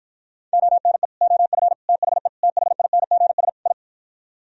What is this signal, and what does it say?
Morse code, heard as a keyed tone, about 52 wpm, 700 Hz. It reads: ONE OF THE THINGS I